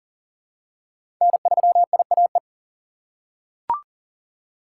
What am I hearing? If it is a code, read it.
Morse code, heard as a keyed tone, about 39 wpm, 700 Hz. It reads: N3IAE
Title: USA Call Signs - 5 Characters - 6x Character Spacing - 40wpm